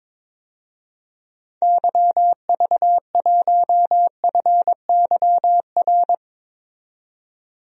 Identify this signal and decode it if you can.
Morse code, heard as a keyed tone, about 21 wpm, 700 Hz. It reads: YV1FYR
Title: DXCC-100 Call Signs - 6 Characters - 22wpm